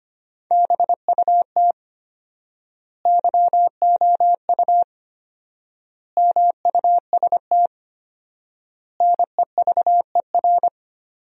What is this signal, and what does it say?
Morse code, heard as a keyed tone, about 25 wpm, 700 Hz. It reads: BUT YOU MUST NEVER
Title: 4x Word-Spacing - Sentences from Top 300 Words - 25wpm